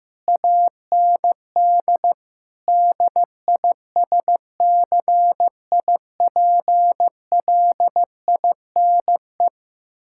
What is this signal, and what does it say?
Morse code, heard as a keyed tone, about 15 wpm, 700 Hz. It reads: AND DISCIPLINE
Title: Sentences from Top 1000 Words - Review Part 6 - 15wpm